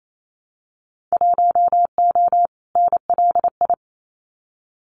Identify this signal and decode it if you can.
Morse code, heard as a keyed tone, about 28 wpm, 700 Hz. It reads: 1O NLI
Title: Winter Field Day Exchanges 28wpm